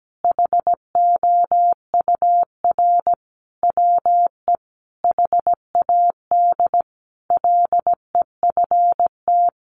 Morse code, heard as a keyed tone, about 17 wpm, 700 Hz. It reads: HOUR WE HAD LEFT
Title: Sentences from Top 1300 Words 17wpm